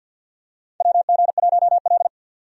Morse code, heard as a keyed tone, about 52 wpm, 700 Hz. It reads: WG1L
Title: USA Call Signs - 4 Characters - 50wpm